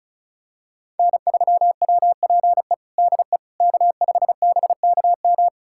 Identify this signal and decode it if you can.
Morse code, heard as a keyed tone, about 34 wpm, 700 Hz. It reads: N3WPE DE K5BKM
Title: USA Callsign DE Callsign - 5 Characters - 35wpm